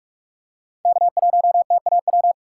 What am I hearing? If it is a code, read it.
Morse code, heard as a keyed tone, about 43 wpm, 700 Hz. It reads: K1TAW
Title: USA Call Signs - 5 Characters - 2x Character Spacing - 45wpm